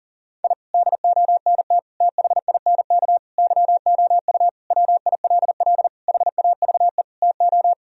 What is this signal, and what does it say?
Morse code, heard as a keyed tone, about 40 wpm, 700 Hz. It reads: I DONT THINK YOU WILL HAVE TO
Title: Sentences from Top 1000 Words - Review Part 4 - 40wpm